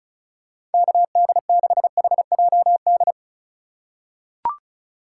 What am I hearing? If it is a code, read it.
Morse code, heard as a keyed tone, about 36 wpm, 700 Hz. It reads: KD6HJD
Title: USA Call Signs - 6 Characters - 4x Character Spacing - 35wpm